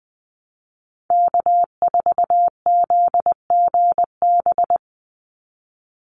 Morse code, heard as a keyed tone, about 20 wpm, 700 Hz. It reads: K4ZGB
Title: Call Signs Repeated 2x - 20wpm